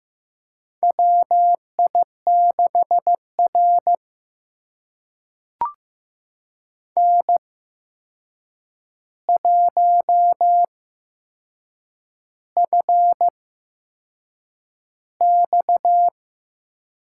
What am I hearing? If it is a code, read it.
Morse code, heard as a keyed tone, about 15 wpm, spaced as for 6 wpm, 700 Hz. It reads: WI6R N1FX
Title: USA Call Signs - 4 Characters - 8x Character Spacing - 15wpm